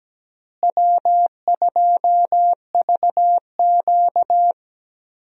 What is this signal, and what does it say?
Morse code, heard as a keyed tone, about 17 wpm, 700 Hz. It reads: W2VQ
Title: USA Call Signs - 4 Characters - Repeated 3x - 17wpm